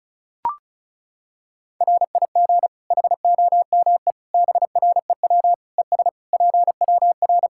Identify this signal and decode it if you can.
Morse code, heard as a keyed tone, about 34 wpm, 700 Hz. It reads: RIG HOME BREW ES PWR
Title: Practice QSO - Exchange 2 - Info - 35wpm